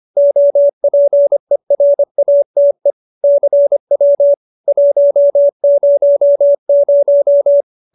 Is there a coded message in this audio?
Yes: Morse code from 0.2 to 7.6 s, heard as a keyed tone, about 25 wpm, 570 Hz. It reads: OPERATE CW 100